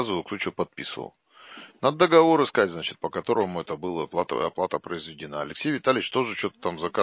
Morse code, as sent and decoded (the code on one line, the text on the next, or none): none